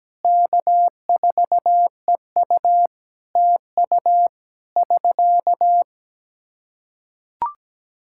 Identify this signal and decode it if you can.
Morse code, heard as a keyed tone, about 17 wpm, 700 Hz. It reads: K4EU TU <SK>
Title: K1USN SST QSO Practice 17wpm